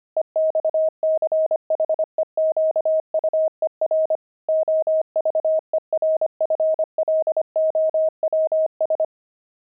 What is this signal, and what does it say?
Morse code, heard as a keyed tone, about 25 wpm, 630 Hz. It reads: EXCHEQUER OVERFLOWS
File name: pangram-fs-25-cs-25-kibitzer